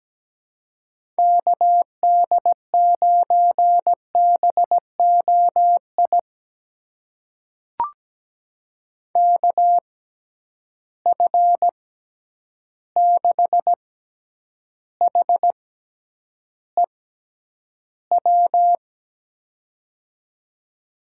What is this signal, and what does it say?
Morse code, heard as a keyed tone, about 17 wpm, spaced as for 9 wpm, 700 Hz. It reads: KD9BOI KF6HEW
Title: USA Call Signs - 6 Characters - 6x Character Spacing - 17wpm